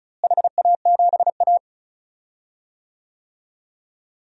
Morse code, heard as a keyed tone, about 36 wpm, 700 Hz. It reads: HA7A
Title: DXCC-100 Call Signs - 4 Characters - Repeated 3x - 35wpm